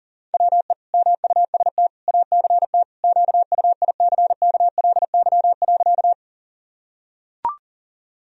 Morse code, heard as a keyed tone, about 40 wpm, 700 Hz. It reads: WE MUST ACT QUICKLY.
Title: Sentences from Top 1000 Words - Review Part 1 - 40wpm